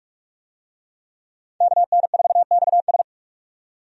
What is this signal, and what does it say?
Morse code, heard as a keyed tone, about 46 wpm, 700 Hz. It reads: KN4XS
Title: USA Call Signs - 5 Characters - 45wpm